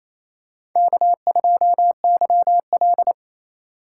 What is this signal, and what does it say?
Morse code, heard as a keyed tone, about 28 wpm, 700 Hz. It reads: K2YL
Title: USA Call Signs - 4 Characters - 4x Character Spacing - 28wpm